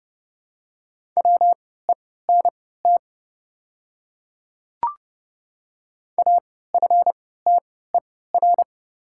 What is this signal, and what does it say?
Morse code, heard as a keyed tone, about 30 wpm, spaced as for 17 wpm, 690 Hz. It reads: WENT AFTER
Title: 3x Character Spacing - Top 300 Words - 30wpm